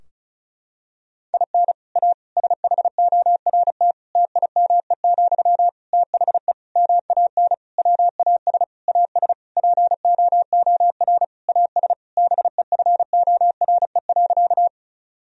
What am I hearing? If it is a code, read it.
Morse code, heard as a keyed tone, about 36 wpm, 700 Hz. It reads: IN A SHORT TIME, THE MAN WAS AS POOR AS BEFORE.
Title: Sentences from Top 1000 Words - Review Part 2 - 35wpm